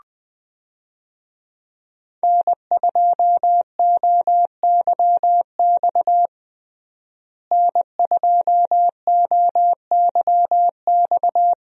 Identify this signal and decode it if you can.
Morse code, heard as a keyed tone, about 21 wpm, 700 Hz. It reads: N2OYX N2OYX
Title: USA Call Signs - 5 Characters - Repeated 2x - 20wpm